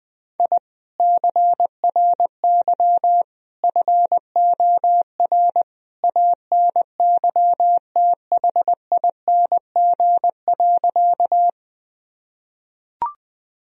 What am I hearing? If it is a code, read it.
Morse code, heard as a keyed tone, about 20 wpm, 700 Hz. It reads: I CRY FOR ANYTHING.
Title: Sentences from Top 1000 Words - Review Part 2 - 20wpm